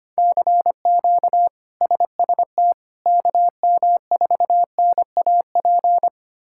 Morse code, heard as a keyed tone, about 25 wpm, 700 Hz. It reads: CQ SST KM4NAP
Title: K1USN SST QSO Practice - 25wpm